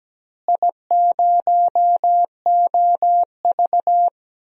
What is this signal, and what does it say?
Morse code, heard as a keyed tone, about 17 wpm, 700 Hz. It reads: I0OV
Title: DXCC-100 Call Signs - 4 Characters - Repeated 2x - 17wpm